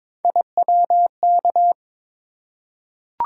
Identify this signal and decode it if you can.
Morse code, heard as a keyed tone, about 23 wpm, 700 Hz. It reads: IWK
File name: practice / 22wpm / Three Letter-Number Combinations - 22wpm